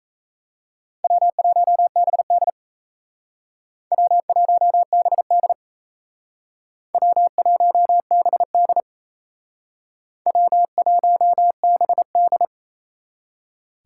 Morse code, heard as a keyed tone, about 36 wpm, 700 Hz. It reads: W1BD W1BD W1BD W1BD